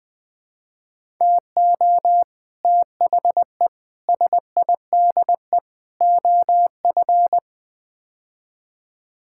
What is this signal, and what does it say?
Morse code, heard as a keyed tone, about 20 wpm, 700 Hz. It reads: TO THE SIDE OF
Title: Build A Phrase - 5-Word Phrases - 20wpm